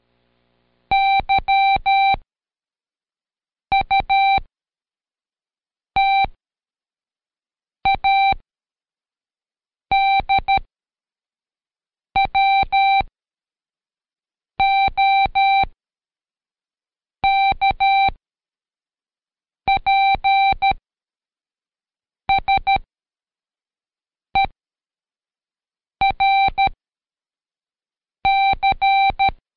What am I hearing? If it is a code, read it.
Morse code, heard as a keyed tone, about 13 wpm, spaced as for 5 wpm, 780 Hz. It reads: YUTADWOKPSERC
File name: Code Quick / AudioDownld2 / Seg95-Day11-FinalReview 2